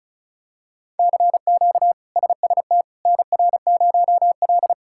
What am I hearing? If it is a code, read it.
Morse code, heard as a keyed tone, about 34 wpm, 700 Hz. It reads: CQ SST NR0L
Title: K1USN SST QSO Practice - 35wpm